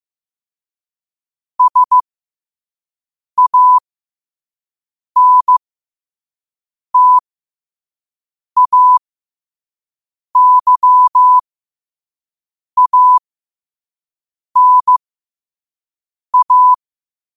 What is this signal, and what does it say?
Morse code, heard as a keyed tone, about 15 wpm, spaced as for 5 wpm, 1000 Hz. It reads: SANTAYANA
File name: qotdfiles / 5wpm